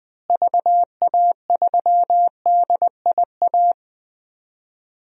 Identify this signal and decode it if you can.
Morse code, heard as a keyed tone, about 21 wpm, 700 Hz. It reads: VA3DIA
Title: DXCC-100 Call Signs - 6 Characters - Repeated 2x - 20wpm